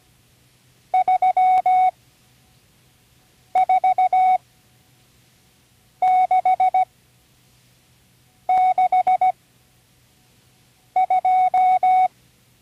Morse code, heard as a keyed tone, about 17 wpm, spaced as for 5 wpm, 740 Hz. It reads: 34662
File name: CW Learning Programs / Learning Morse Code (from Tune in the World) / 35 answers